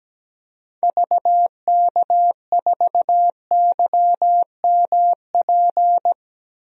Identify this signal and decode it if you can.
Morse code, heard as a keyed tone, about 17 wpm, 700 Hz. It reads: VK4YMP
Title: DXCC-100 Call Signs - 6 Characters - 17wpm